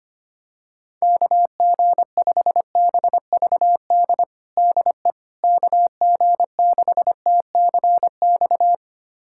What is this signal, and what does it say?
Morse code, heard as a keyed tone, about 25 wpm, 700 Hz. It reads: KG5BVD DE KG6TCX
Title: USA Callsign DE Callsign 25wpm